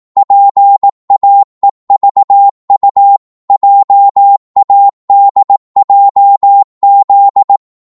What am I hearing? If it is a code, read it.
Morse code, heard as a keyed tone, about 18 wpm, 800 Hz. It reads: PAEVU JADJZ
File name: Morse / ltr-18wpm